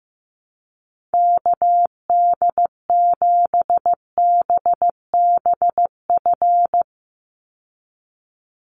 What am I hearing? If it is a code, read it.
Morse code, heard as a keyed tone, about 15 wpm, 700 Hz. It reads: KD7BBF
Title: USA Call Signs 15wpm